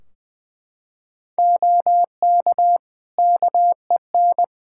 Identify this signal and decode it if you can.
Morse code, heard as a keyed tone, about 20 wpm, 700 Hz. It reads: OK KEN
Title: Practice QSO - Exchange 3 - More Info - 20wpm